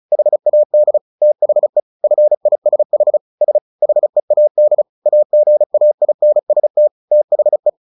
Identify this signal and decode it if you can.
Morse code, heard as a keyed tone, about 34 wpm, 600 Hz. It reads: HAD THE FISH S HEAD AGAINST THE
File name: the old man and the sea 2 35wpm